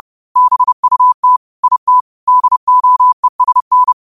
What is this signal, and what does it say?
Morse code, heard as a keyed tone, about 30 wpm, 1000 Hz. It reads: BUT IT DOESN
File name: qotdfiles / 30wpm